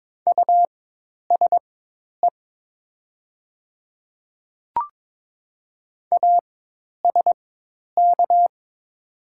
Morse code, heard as a keyed tone, about 23 wpm, spaced as for 10 wpm, 700 Hz. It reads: USE ASK
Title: 4x Character Spacing - Top 200 Words - 22wpm